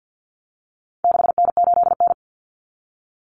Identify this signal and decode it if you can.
Morse code, heard as a keyed tone, about 49 wpm, 700 Hz. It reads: 6N8N
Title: Common Cut Numbers - 4-Digit 50wpm